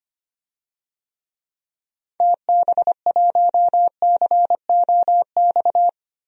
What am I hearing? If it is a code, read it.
Morse code, heard as a keyed tone, about 25 wpm, 700 Hz. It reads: TB1COX
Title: DXCC-100 Call Signs - 6 Characters - 25wpm